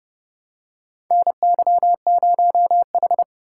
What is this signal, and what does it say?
Morse code, heard as a keyed tone, about 30 wpm, 700 Hz. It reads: NY0H
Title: USA Call Signs - 4 Characters - 2x Character Spacing - 30wpm